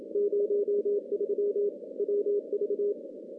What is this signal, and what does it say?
Morse code, heard as a keyed tone, about 25 wpm, 410 Hz. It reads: 03 WV
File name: contests / CW-SS-k5tr- -2300